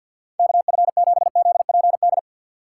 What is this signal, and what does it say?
Morse code, heard as a keyed tone, about 53 wpm, 700 Hz. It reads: KF7ZPD
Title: USA Call Signs - 6 Characters - 4x Character Spacing - 50wpm